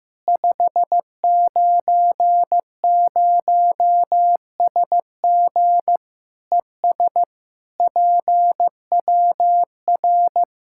Morse code, heard as a keyed tone, about 15 wpm, 700 Hz. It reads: 590SG ES PWR
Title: Practice QSO - Exchange 2 - Info - 15wpm